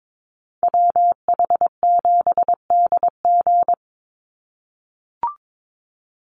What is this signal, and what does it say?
Morse code, heard as a keyed tone, about 22 wpm, 700 Hz. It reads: WH7DG
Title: USA Call Signs 22wpm